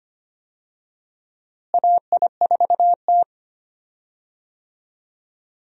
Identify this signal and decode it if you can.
Morse code, heard as a keyed tone, about 25 wpm, 700 Hz. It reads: AI4T
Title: USA Call Signs - 4 Characters - Repeated 3x - 25wpm